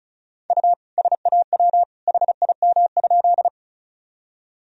Morse code, heard as a keyed tone, about 36 wpm, 700 Hz. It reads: U SAW HIM?
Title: Sentences from Top 300 Words and Abbreviations 35wpm